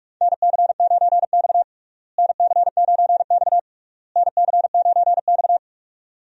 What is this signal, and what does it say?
Morse code, heard as a keyed tone, about 41 wpm, 700 Hz. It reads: NC9X NC9X NC9X
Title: USA Call Signs - 4 Characters - Repeated 3x - 45wpm